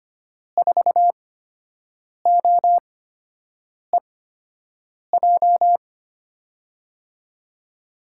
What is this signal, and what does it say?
Morse code, heard as a keyed tone, about 25 wpm, spaced as for 7 wpm, 700 Hz. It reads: 4OEJ
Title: USA Call Signs - 8x Character Spacing - 25wpm